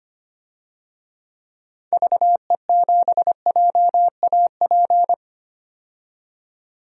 Morse code, heard as a keyed tone, about 25 wpm, 700 Hz. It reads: VE7JAP
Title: DXCC-100 Call Signs - 6 Characters - 25wpm